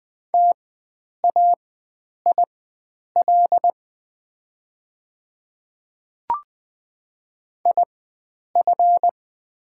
Morse code, heard as a keyed tone, about 20 wpm, spaced as for 9 wpm, 700 Hz. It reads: TAIL IF
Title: 4x Character Spacing - Top 500 Words - 20wpm